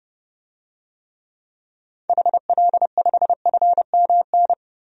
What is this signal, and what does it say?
Morse code, heard as a keyed tone, about 30 wpm, 700 Hz. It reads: HL5FMN
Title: DXCC-100 Call Signs - 6 Characters - 30wpm